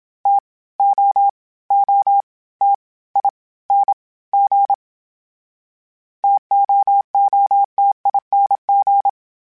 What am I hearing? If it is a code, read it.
Morse code, heard as a keyed tone, about 27 wpm, 800 Hz. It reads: T O O T I N G TOOTING
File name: Variable Spacing / Variable Spacing-mp3s / T AOEs @25